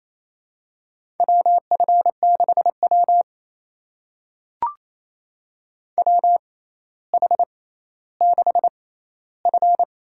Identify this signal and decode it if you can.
Morse code, heard as a keyed tone, about 28 wpm, spaced as for 14 wpm, 700 Hz. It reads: WF6W WH6F
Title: USA Call Signs - 4 Characters - 6x Character Spacing - 28wpm